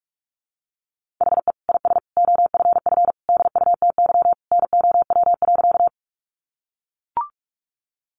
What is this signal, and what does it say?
Morse code, heard as a keyed tone, about 44 wpm, 700 Hz. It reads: HE IS OFF DUTY NOW.
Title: Sentences from Top 1500 Words 45wpm